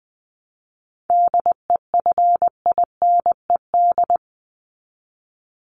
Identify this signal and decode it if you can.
Morse code, heard as a keyed tone, about 20 wpm, 700 Hz. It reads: DEFINED
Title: ED Suffix - Top 100 Words with Base Words 20wpm